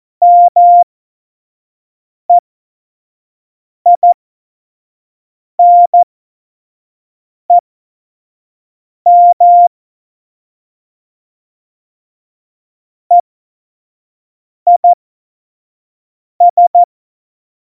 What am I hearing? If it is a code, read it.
Morse code, heard as a keyed tone, about 14 wpm, spaced as for 5 wpm, 700 Hz. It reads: MEINEM EIS
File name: Morsekurs / Witze / Witz 03. September 2019, 25 F70